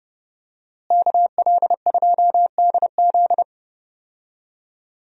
Morse code, heard as a keyed tone, about 30 wpm, 700 Hz. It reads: KL2DZ
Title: USA Call Signs - 5 Characters - Repeated 2x - 30wpm